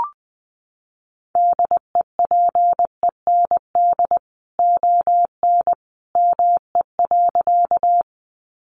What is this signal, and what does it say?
Morse code, heard as a keyed tone, about 20 wpm, 700 Hz. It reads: DEPEND ON ME.
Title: Sentences from Top 1300 Words 20wpm